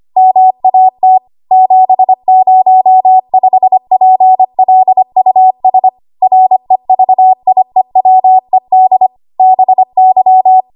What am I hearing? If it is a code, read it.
Morse code, heard as a keyed tone, about 25 wpm, 750 Hz. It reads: MAT 705PLUS REVIEWED BY
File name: Archive / 25 WPM / 25WPM